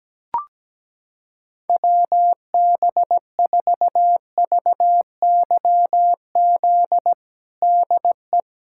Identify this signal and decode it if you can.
Morse code, heard as a keyed tone, about 17 wpm, 700 Hz. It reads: WB4VYZ DE
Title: USA Callsign DE Callsign 17wpm